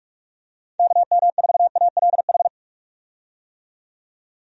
Morse code, heard as a keyed tone, about 46 wpm, 700 Hz. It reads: KM4ALH
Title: USA Call Signs - 6 Characters - 45wpm